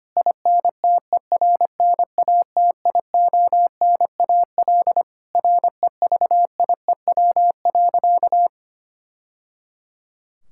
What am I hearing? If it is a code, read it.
Morse code, heard as a keyed tone, about 25 wpm, 700 Hz. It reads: INTERNATIONAL REVIEW.
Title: Sentences from Top 1000 Words - Review Part 6 - 25wpm